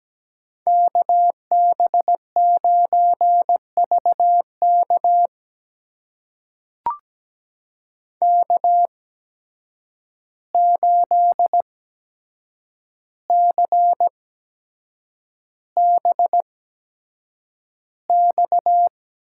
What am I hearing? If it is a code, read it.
Morse code, heard as a keyed tone, about 17 wpm, spaced as for 7 wpm, 700 Hz. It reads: KB9VK K8CBX
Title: USA Call Signs - 5 Characters - 8x Character Spacing - 17wpm